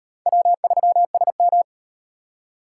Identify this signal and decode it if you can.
Morse code, heard as a keyed tone, about 37 wpm, 700 Hz. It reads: W3SM